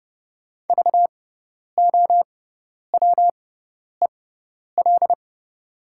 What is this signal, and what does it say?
Morse code, heard as a keyed tone, about 30 wpm, spaced as for 10 wpm, 700 Hz. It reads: VOWEL
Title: 6x Character Spacing - Top 500 Words - 30wpm